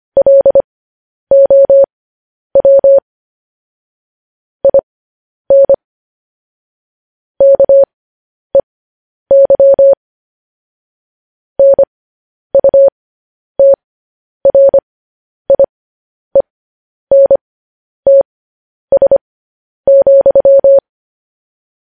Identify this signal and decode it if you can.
Morse code, heard as a keyed tone, about 25 wpm, spaced as for 10 wpm, 550 Hz. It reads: LOW IN KEY NUTRIENTS,
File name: podcast / News headlines-1x-25x10